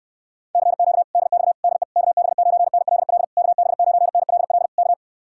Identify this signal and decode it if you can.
Morse code, heard as a keyed tone, about 59 wpm, 700 Hz. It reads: CQ DX DE KB9NLL KB9NLL K